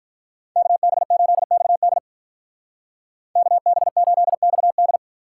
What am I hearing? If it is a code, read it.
Morse code, heard as a keyed tone, about 43 wpm, 700 Hz. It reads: KB8XD KB8XD
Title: Speed Racing - USA Call Signs 35wpm